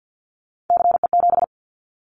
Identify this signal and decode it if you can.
Morse code, heard as a keyed tone, about 50 wpm, 700 Hz. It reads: CE7
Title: Callsign Prefixes for Top 100 DXCC - 50wpm